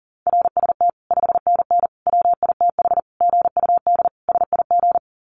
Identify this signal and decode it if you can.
Morse code, heard as a keyed tone, about 40 wpm, 700 Hz. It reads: RST 5NN WITH GUD SIG
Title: Common QSO Phrases 40wpm